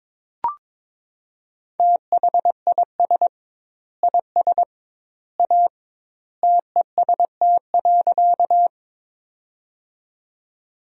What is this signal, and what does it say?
Morse code, heard as a keyed tone, about 22 wpm, 700 Hz. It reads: THIS IS A TEST.
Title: 2x Word-Spacing - Sentences from Top 500 Words - 22wpm